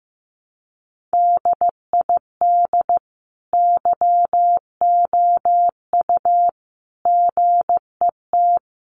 Morse code, heard as a keyed tone, about 15 wpm, 700 Hz. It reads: DID YOU GET
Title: Sentences from Top 1100 Words 15wpm